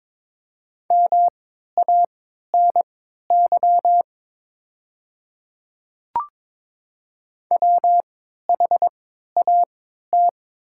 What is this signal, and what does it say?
Morse code, heard as a keyed tone, about 22 wpm, spaced as for 13 wpm, 690 Hz. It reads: MANY WHAT
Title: 3x Character Spacing - Top 100 Words - 22wpm